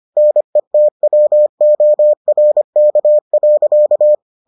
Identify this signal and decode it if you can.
Morse code, heard as a keyed tone, about 25 wpm, 600 Hz. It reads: NETWORK.